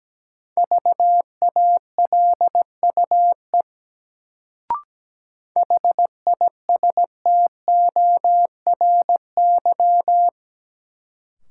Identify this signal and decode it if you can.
Morse code, heard as a keyed tone, about 17 wpm, 700 Hz. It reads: VALUE HISTORY
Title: Sentences from Top 600 Words - New Word Drill - 17wpm